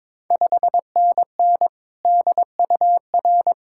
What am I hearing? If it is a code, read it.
Morse code, heard as a keyed tone, about 22 wpm, 700 Hz. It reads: 5NN DUR
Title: Ontario QSO Party Exchange 22wpm